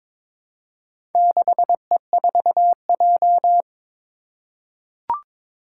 Morse code, heard as a keyed tone, about 22 wpm, 700 Hz. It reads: 6E4J
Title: Four Challenging Letter-Number Combinations 22wpm